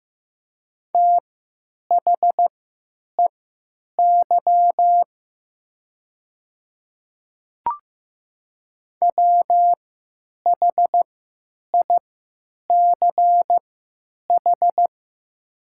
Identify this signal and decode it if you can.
Morse code, heard as a keyed tone, about 15 wpm, spaced as for 9 wpm, 700 Hz. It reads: THEY WHICH
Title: 3x Character Spacing - Top 100 Words - 15wpm